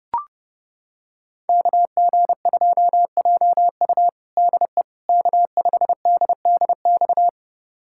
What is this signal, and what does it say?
Morse code, heard as a keyed tone, about 30 wpm, 700 Hz. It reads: KG2JU DE K5DDX
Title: USA Callsign DE Callsign - 5 Characters - 30wpm